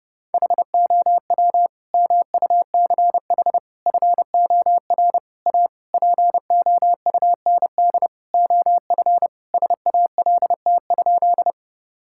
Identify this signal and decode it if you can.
Morse code, heard as a keyed tone, about 29 wpm, 700 Hz. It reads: HOW MUCH FOR A POUND OF SALT?
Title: Sentences from Top 1000 Words - Review Part 5 - 30wpm